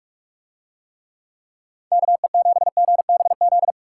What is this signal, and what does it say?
Morse code, heard as a keyed tone, about 46 wpm, 700 Hz. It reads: KE7GBZ